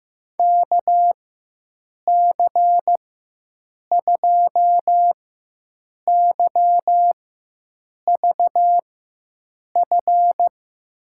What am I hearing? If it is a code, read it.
Morse code, heard as a keyed tone, about 15 wpm, spaced as for 7 wpm, 700 Hz. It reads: KC2YVF
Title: USA Call Signs - 4x Character Spacing - 15wpm